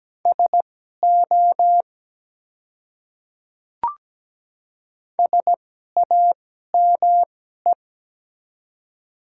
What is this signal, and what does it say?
Morse code, heard as a keyed tone, about 17 wpm, spaced as for 12 wpm, 700 Hz. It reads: SO SAME
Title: 2x Character Spacing - Top 200 Words - 17wpm